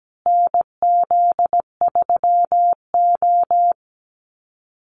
Morse code, heard as a keyed tone, about 17 wpm, 700 Hz. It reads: NZ3O
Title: Call Signs Repeated 2x - 17wpm